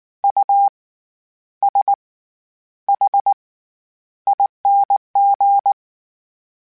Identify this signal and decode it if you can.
Morse code, heard as a keyed tone, about 19 wpm, spaced as for 10 wpm, 800 Hz. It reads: USHING
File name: Special INGs / Special INGs-mp3s / SlowToFaster ing @18